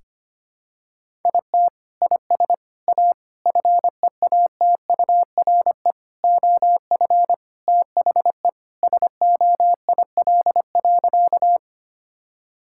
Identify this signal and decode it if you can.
Morse code, heard as a keyed tone, about 25 wpm, 700 Hz. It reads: IT IS A FEATURE OF THE SOIL.
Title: Sentences from Top 1000 Words - Review Part 5 - 25wpm